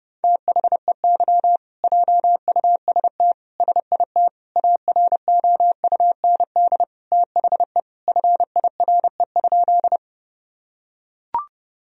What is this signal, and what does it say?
Morse code, heard as a keyed tone, about 30 wpm, 700 Hz. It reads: THEY JUST SIT AROUND THE FIRE?
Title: Sentences from Top 1000 Words - Review Part 3 - 30wpm